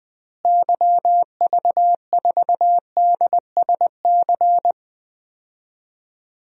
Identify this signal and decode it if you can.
Morse code, heard as a keyed tone, about 20 wpm, 700 Hz. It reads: YV4DSC
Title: DXCC-100 Call Signs - 6 Characters - Repeated 2x - 20wpm